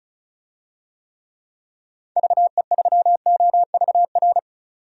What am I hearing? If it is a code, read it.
Morse code, heard as a keyed tone, about 36 wpm, 700 Hz. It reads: VE3OVR
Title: DXCC-100 Call Signs - 6 Characters - 35wpm